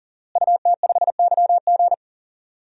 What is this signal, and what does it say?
Morse code, heard as a keyed tone, about 40 wpm, 700 Hz. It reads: UT5YG
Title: DXCC-100 Call Signs - 5 Characters - Repeated 3x - 40wpm